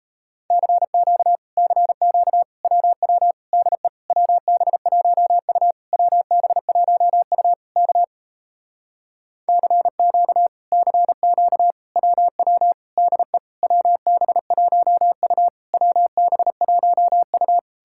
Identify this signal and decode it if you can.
Morse code, heard as a keyed tone, about 37 wpm, 700 Hz. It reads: CQ CQ WW DE WB1U WB1U K CQ CQ WW DE WB1U WB1U